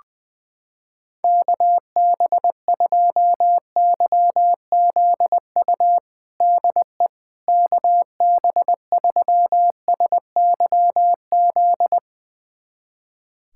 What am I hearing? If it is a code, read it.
Morse code, heard as a keyed tone, about 21 wpm, 700 Hz. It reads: KB2YZU DE KB3SYZ
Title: USA Callsign DE Callsign 20wpm